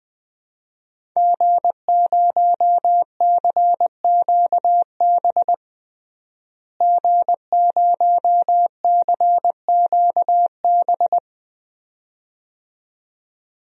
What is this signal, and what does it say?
Morse code, heard as a keyed tone, about 20 wpm, 700 Hz. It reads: G0CQB G0CQB
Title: DXCC-100 Call Signs - 5 Characters - Repeated 2x - 20wpm